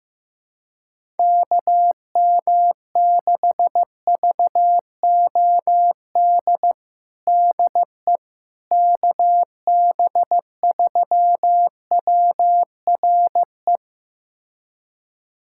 Morse code, heard as a keyed tone, about 15 wpm, 700 Hz. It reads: KM6VOD DE KB3WRE
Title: USA Callsign DE Callsign 15wpm